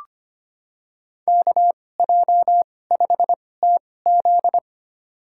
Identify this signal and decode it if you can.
Morse code, heard as a keyed tone, about 25 wpm, spaced as for 18 wpm, 700 Hz. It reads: KJ5TZ
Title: USA Call Signs - 5 Characters - 2x Character Spacing - 25wpm